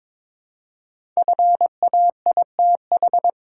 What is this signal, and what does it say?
Morse code, heard as a keyed tone, about 22 wpm, 700 Hz. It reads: FAITH
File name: 5 Letter Words - First 3 Letters Given - 22wpm